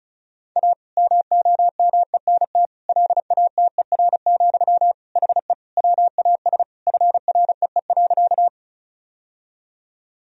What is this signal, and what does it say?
Morse code, heard as a keyed tone, about 36 wpm, 700 Hz. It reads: A MOMENT LATER, HE WAS FREE.
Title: Sentences from Top 1000 Words - Review Part 4 - 35wpm